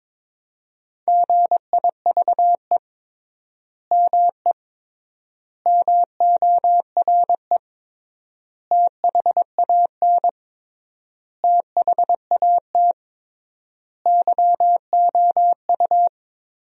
Morse code, heard as a keyed tone, about 21 wpm, 700 Hz. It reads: GIVE ME MORE THAN THAT YOU
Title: 3x Word-Spacing - Sentences from Top 200 Words - 22wpm